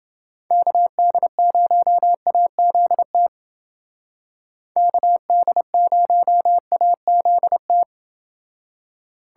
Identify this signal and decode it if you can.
Morse code, heard as a keyed tone, about 29 wpm, 700 Hz. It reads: KD0AZT KD0AZT